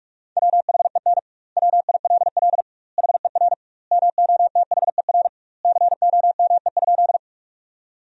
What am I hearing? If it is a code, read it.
Morse code, heard as a keyed tone, about 46 wpm, 700 Hz. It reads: WHEN WILL HER MOTHER COME?
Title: 2x Word-Spacing - Sentences from Top 200 Words - 45wpm